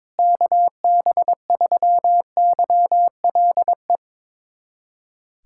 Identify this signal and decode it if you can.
Morse code, heard as a keyed tone, about 22 wpm, 700 Hz. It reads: KB3YLE
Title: USA Callsign DE Callsign - 6 Characters - 22wpm